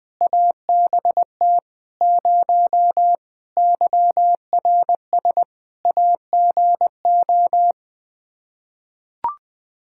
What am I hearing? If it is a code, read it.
Morse code, heard as a keyed tone, about 20 wpm, 700 Hz. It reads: ABT 0 YRS AGO
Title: Practice QSO - Exchange 3 - More Info - 20wpm